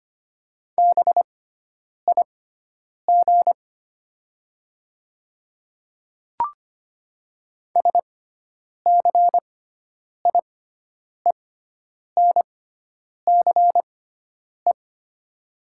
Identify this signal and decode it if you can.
Morse code, heard as a keyed tone, about 25 wpm, spaced as for 9 wpm, 700 Hz. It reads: BIG SCIENCE
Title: 6x Character Spacing - Top 500 Words - 25wpm